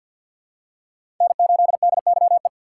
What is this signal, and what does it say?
Morse code, heard as a keyed tone, about 50 wpm, 700 Hz. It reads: N8DYE
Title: USA Call Signs - 5 Characters - 2x Character Spacing - 50wpm